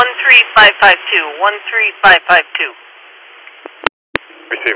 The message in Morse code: none

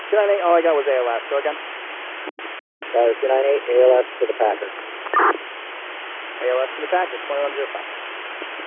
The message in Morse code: none

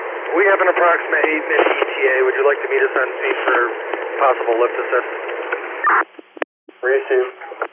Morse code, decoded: none

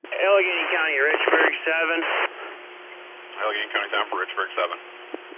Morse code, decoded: none

